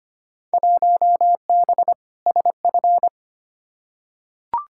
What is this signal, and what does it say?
Morse code, heard as a keyed tone, about 25 wpm, 700 Hz. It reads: 1B SF
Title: Field Day Exchanges Encore - 25wpm